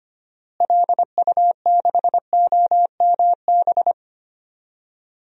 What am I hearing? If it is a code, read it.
Morse code, heard as a keyed tone, about 25 wpm, 700 Hz. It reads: LU6OMB
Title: DXCC-100 Call Signs - 6 Characters - Repeated 3x - 25wpm